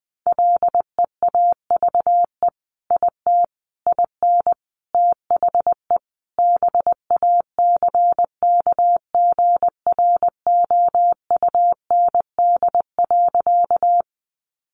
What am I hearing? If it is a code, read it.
Morse code, heard as a keyed tone, about 20 wpm, 700 Hz. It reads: LEAVE IT IN THE BACKGROUND.
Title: Sentences from Top 1300 Words 20wpm